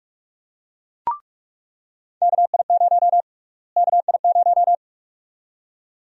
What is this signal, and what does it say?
Morse code, heard as a keyed tone, about 47 wpm, 700 Hz. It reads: KI0 KI0
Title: USA Call Sign Prefixes - Repeated 2x - 45wpm